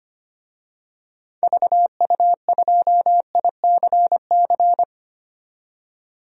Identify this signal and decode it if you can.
Morse code, heard as a keyed tone, about 25 wpm, 700 Hz. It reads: VU2ICC
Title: DXCC-100 Call Signs - 6 Characters - 25wpm